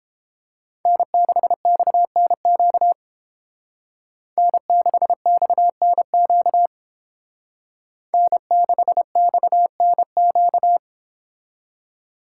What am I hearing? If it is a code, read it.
Morse code, heard as a keyed tone, about 28 wpm, 700 Hz. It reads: N6XNQ N6XNQ N6XNQ